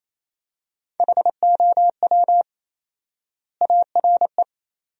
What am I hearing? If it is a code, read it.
Morse code, heard as a keyed tone, about 28 wpm, 700 Hz. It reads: HOW ARE
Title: 4x Word-Spacing - Sentences from Top 100 Words - 28wpm